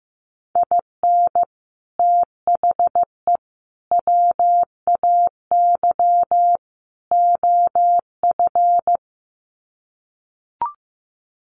Morse code, heard as a keyed tone, about 15 wpm, 700 Hz. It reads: IN THE WAY OF
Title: Sets of 4 Words 15wpm